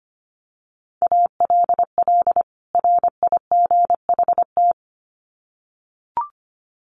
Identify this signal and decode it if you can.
Morse code, heard as a keyed tone, about 25 wpm, 700 Hz. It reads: ALL RIGHT
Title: Sets of 2 Words 25wpm